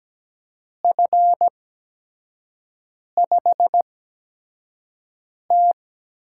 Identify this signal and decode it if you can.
Morse code, heard as a keyed tone, about 17 wpm, spaced as for 5 wpm, 700 Hz. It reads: F5T
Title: USA Call Signs - 6 Characters - 8x Character Spacing - 17wpm